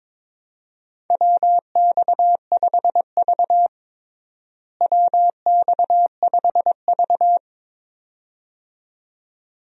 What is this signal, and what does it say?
Morse code, heard as a keyed tone, about 22 wpm, 700 Hz. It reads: WX5V WX5V
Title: USA Call Signs - 4 Characters - Repeated 2x - 22wpm